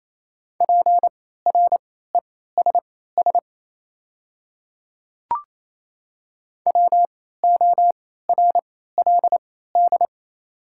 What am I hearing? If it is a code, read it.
Morse code, heard as a keyed tone, about 28 wpm, spaced as for 16 wpm, 700 Hz. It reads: PRESS WORLD
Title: 3x Character Spacing - Top 500 Words - 28wpm